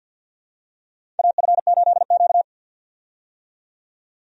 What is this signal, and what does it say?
Morse code, heard as a keyed tone, about 53 wpm, 700 Hz. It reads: AF8Q
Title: USA Call Signs - 4 Characters - 50wpm